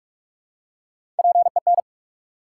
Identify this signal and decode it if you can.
Morse code, heard as a keyed tone, about 47 wpm, 700 Hz. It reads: PEN
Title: Three Letter Words - 45wpm